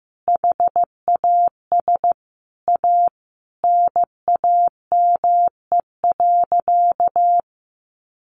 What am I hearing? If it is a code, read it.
Morse code, heard as a keyed tone, about 15 wpm, 700 Hz. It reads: HAS A NAME.
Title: Sentences from Top 500 Words 15wpm